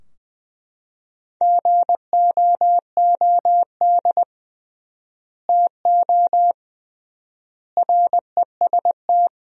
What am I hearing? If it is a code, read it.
Morse code, heard as a keyed tone, about 20 wpm, 700 Hz. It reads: GOOD TO REST
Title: 3x Word-Spacing - Sentences from Top 500 Words - 20wpm